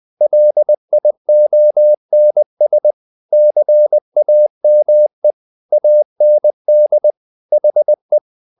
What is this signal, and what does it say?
Morse code, heard as a keyed tone, about 20 wpm, 600 Hz. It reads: LIONS CAME AND HE
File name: the old man and the sea part 2 20wpm